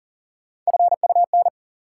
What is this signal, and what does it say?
Morse code, heard as a keyed tone, about 40 wpm, 700 Hz. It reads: FUN
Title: Sentences from Top 700 Words - New Word Drill - 40wpm